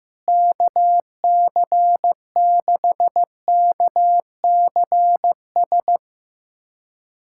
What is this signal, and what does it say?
Morse code, heard as a keyed tone, about 15 wpm, 700 Hz. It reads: KC6KCS
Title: USA Callsign DE Callsign - 6 Characters - 15wpm